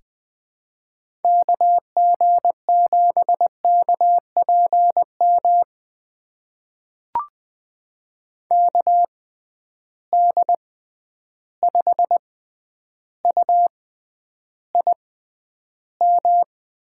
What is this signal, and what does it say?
Morse code, heard as a keyed tone, about 20 wpm, spaced as for 10 wpm, 700 Hz. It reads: KG7KPM KD5UIM
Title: USA Call Signs - 6 Characters - 6x Character Spacing - 20wpm